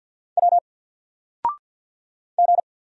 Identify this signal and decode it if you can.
Morse code, heard as a keyed tone, about 50 wpm, 700 Hz. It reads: W G
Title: Single Letter-Number 50wpm